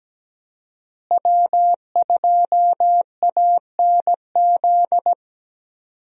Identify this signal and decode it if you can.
Morse code, heard as a keyed tone, about 17 wpm, 700 Hz. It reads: W2ANZ
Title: USA Call Signs - 5 Characters - Repeated 3x - 17wpm